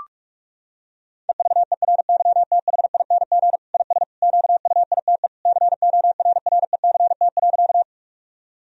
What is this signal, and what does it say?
Morse code, heard as a keyed tone, about 42 wpm, 700 Hz. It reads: EVERYTHING IS QUITE CORRECT.
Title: Sentences from Top 1000 Words - Review Part 2 - 45wpm